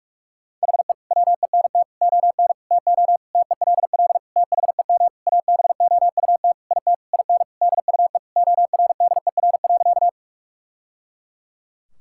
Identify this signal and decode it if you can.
Morse code, heard as a keyed tone, about 46 wpm, 700 Hz. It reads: HE WENT ON TO TELL THEM ABOUT IT IN DUE ORDER.
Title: Sentences from Top 1000 Words - Review Part 6 - 45wpm